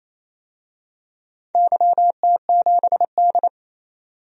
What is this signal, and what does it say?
Morse code, heard as a keyed tone, about 28 wpm, 700 Hz. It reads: YT7D
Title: DXCC-100 Call Signs - 4 Characters - 28wpm